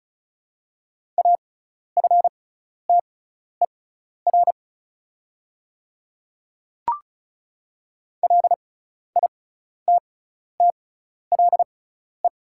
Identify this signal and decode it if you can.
Morse code, heard as a keyed tone, about 36 wpm, spaced as for 12 wpm, 700 Hz. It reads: AFTER LITTLE
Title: 6x Character Spacing - Top 200 Words - 35wpm